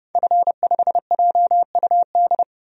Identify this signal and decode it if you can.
Morse code, heard as a keyed tone, about 31 wpm, 700 Hz. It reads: F5JUD
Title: DXCC-100 Call Signs - 5 Characters - 30wpm